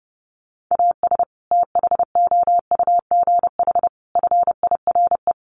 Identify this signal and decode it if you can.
Morse code, heard as a keyed tone, about 30 wpm, 700 Hz. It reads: AS THOUGH FIRE
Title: Sentences from Top 1500 Words 30wpm